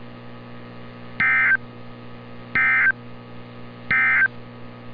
Emergency Alert System (SAME) tones, heard as harsh buzzing bursts. It is heard from 1.1 to 4.4 s.
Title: NOAA Weather Radio - Altoona, PA